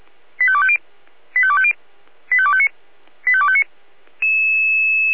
Five-tone selective call (ZVEI), heard as quick runs of five tones, tones of about 70 ms. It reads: 85250 85250 85259 85259